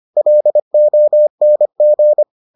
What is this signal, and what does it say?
Morse code, heard as a keyed tone, about 25 wpm, 600 Hz. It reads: LONG